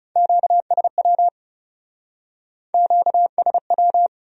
Morse code, heard as a keyed tone, about 37 wpm, 700 Hz. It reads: QSW QSW
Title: Speed-Racing Q-Codes - 30wpm